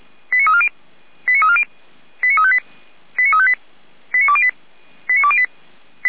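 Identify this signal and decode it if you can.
Five-tone selective call (ZVEI), heard as quick runs of five tones, tones of about 80 ms. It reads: 89340 89340 89358 89358 89208 89208
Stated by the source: Five-tone ZVEI selective call received by SDR, from audio server of Feuerwehr Orlamünde (fire brigade), code 89315.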